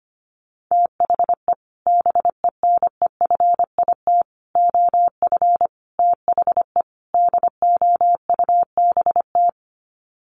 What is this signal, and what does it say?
Morse code, heard as a keyed tone, about 25 wpm, 690 Hz. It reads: THE BENEFIT OF THE DOUBT
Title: Sets of 5 Words Encore 25wpm